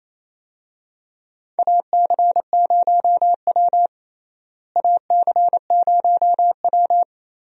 Morse code, heard as a keyed tone, about 28 wpm, 700 Hz. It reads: AC0W AC0W
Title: USA Call Signs - 4 Characters - Repeated 3x - 28wpm